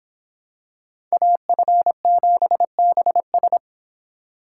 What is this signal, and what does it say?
Morse code, heard as a keyed tone, about 26 wpm, 700 Hz. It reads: AF7BS